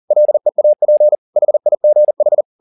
Morse code, heard as a keyed tone, about 40 wpm, 600 Hz. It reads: LEAP HIGH
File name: the old man and the sea 2 40wpm